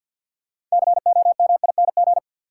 Morse code, heard as a keyed tone, about 53 wpm, 700 Hz. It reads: COMING
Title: Sentences from Top 1300 Words - New Word Drill Alt - 50wpm